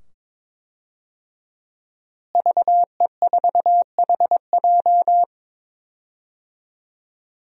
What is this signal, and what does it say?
Morse code, heard as a keyed tone, about 22 wpm, 700 Hz. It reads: VE4HJ
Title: DXCC-100 Call Signs - 5 Characters - Repeated 2x - 22wpm